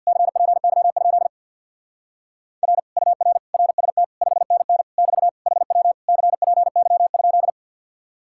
Morse code, heard as a keyed tone, about 61 wpm, 700 Hz. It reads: CPY? R UR RST 5NN <BT> HW CPY?
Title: Speed-Racing Common QSO Phrases Encore - 50wpm